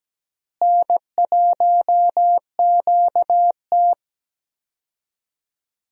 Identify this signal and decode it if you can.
Morse code, heard as a keyed tone, about 17 wpm, 700 Hz. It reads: N1QT
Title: USA Call Signs - 4 Characters - Repeated 2x - 17wpm